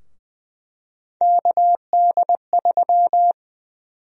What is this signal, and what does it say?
Morse code, heard as a keyed tone, about 20 wpm, 700 Hz. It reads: KD3
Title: USA Call Sign Prefixes - 20wpm